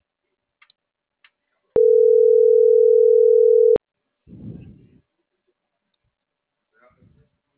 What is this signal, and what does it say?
Ringback tone (North America), 440 + 480 Hz.